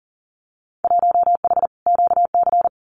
Morse code, heard as a keyed tone, about 40 wpm, 700 Hz. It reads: 1H QC
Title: Winter Field Day Exchanges 40wpm